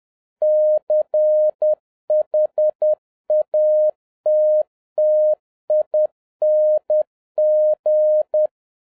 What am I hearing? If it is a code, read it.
Morse code, heard as a keyed tone, about 10 wpm, 610 Hz. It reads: CHATTING
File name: 2011-07-mp3 / n56 BACK FROM A BIG ACCIDENT 10wpm